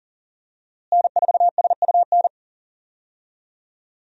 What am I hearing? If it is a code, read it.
Morse code, heard as a keyed tone, about 40 wpm, 700 Hz. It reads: N4SUN
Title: USA Call Signs - 5 Characters - Repeated 3x - 40wpm